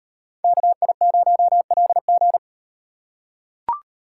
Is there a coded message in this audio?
Yes: Morse code from 0.4 to 2.4 s, heard as a keyed tone, about 38 wpm, 700 Hz. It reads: KI0LG